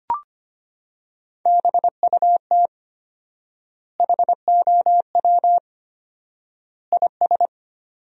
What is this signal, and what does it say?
Morse code, heard as a keyed tone, about 25 wpm, 700 Hz. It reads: BUT HOW IS
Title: 4x Word-Spacing - Sentences from Top 100 Words - 25wpm